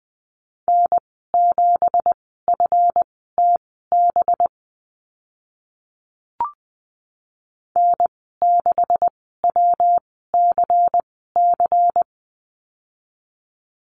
Morse code, heard as a keyed tone, about 20 wpm, spaced as for 15 wpm, 700 Hz. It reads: N7FTB N6WCC
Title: 186 Lesson 30g - Letter b - Call Signs 20wpm